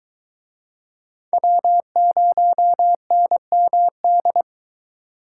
Morse code, heard as a keyed tone, about 24 wpm, 700 Hz. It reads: W0NMD